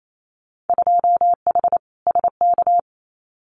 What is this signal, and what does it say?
Morse code, heard as a keyed tone, about 28 wpm, 700 Hz. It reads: 2H SK
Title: Winter Field Day Exchanges Encore 28wpm